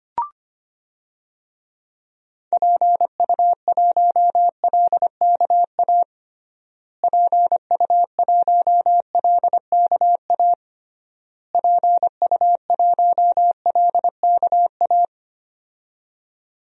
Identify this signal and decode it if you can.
Morse code, heard as a keyed tone, about 25 wpm, 700 Hz. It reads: PU1LKA PU1LKA PU1LKA
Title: DXCC-100 Call Signs - 6 Characters - Repeated 3x - 25wpm